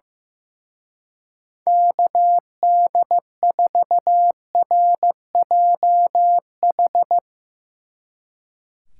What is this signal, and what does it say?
Morse code, heard as a keyed tone, about 16 wpm, 700 Hz. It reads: KD4RJH
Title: Speed Racing - USA 2x3 Call Signs 15wpm